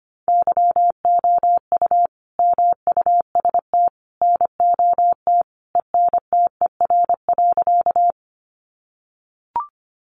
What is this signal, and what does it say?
Morse code, heard as a keyed tone, about 25 wpm, 700 Hz. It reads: YOU MUST NOT ENTER.
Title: Sentences from Top 1300 Words 25wpm